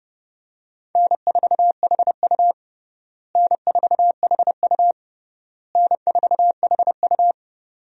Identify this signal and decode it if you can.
Morse code, heard as a keyed tone, about 32 wpm, 700 Hz. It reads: N4HU N4HU N4HU
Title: USA Call Signs - 4 Characters - Repeated 3x - 30wpm